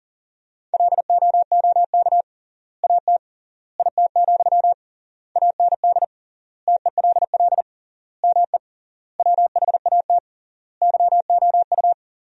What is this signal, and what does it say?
Morse code, heard as a keyed tone, about 40 wpm, 700 Hz. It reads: LOOK AT IT, AND TELL ME WHAT YOU
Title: 3x Word-Spacing - Sentences from Top 200 Words - 40wpm